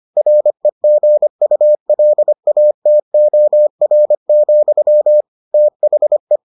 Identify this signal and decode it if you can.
Morse code, heard as a keyed tone, about 25 wpm, 600 Hz. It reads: REGULATOR, THE